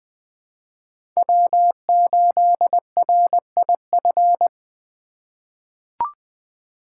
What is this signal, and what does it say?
Morse code, heard as a keyed tone, about 20 wpm, 700 Hz. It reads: W8RIF
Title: USA Call Signs - 2x Character Spacing - 20wpm